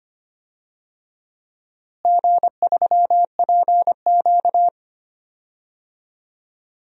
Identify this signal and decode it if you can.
Morse code, heard as a keyed tone, about 25 wpm, 700 Hz. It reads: G3PQ
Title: DXCC-100 Call Signs - 4 Characters - Repeated 3x - 25wpm